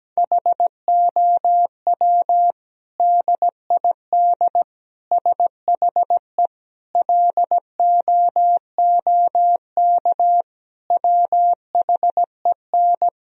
Morse code, heard as a keyed tone, about 17 wpm, 700 Hz. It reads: HOW DID SHE LOOK WHEN
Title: Sentences from Top 1000 Words - Review Part 3 - 17wpm